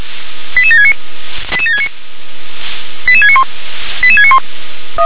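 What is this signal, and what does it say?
Five-tone selective call (ZVEI), heard as quick runs of five tones, tones of about 70 ms. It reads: 88769 88769 88691 88691
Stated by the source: ZVEI five-tone call 88690 received by SDR.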